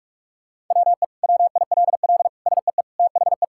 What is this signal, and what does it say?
Morse code, heard as a keyed tone, about 46 wpm, 700 Hz. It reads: WE WILL SEE THE
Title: Sentences from Top 1000 Words - Review Part 5 - 45wpm